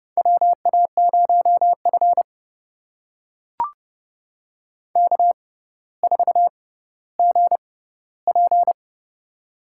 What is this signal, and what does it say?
Morse code, heard as a keyed tone, about 32 wpm, spaced as for 16 wpm, 700 Hz. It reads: WA0F K4GP
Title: USA Call Signs - 4 Characters - 6x Character Spacing - 30wpm